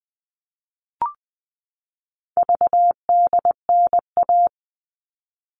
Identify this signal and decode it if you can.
Morse code, heard as a keyed tone, about 20 wpm, 700 Hz. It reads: VDNA
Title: All Cut Numbers - 4-Digit 20wpm